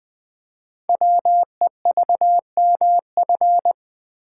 Morse code, heard as a keyed tone, about 20 wpm, 700 Hz. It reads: WEVMF
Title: Code Groups - 5 Letters - 20wpm